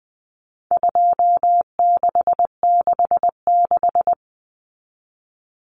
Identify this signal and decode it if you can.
Morse code, heard as a keyed tone, about 20 wpm, 700 Hz. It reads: 2666
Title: Common Cut Numbers - 4-Digit 20wpm